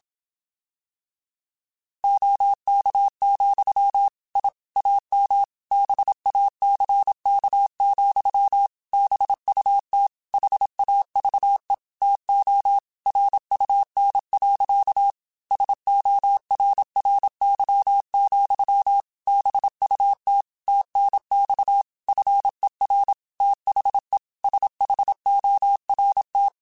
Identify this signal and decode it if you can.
Morse code, heard as a keyed tone, about 27 wpm, 800 Hz. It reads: OK, I AM BACK, BUT HAVE TO RUN. SORRY, BUT TNX FER THE SHORT
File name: QSOs / QSOs-mp3s / WA7QOX 2@25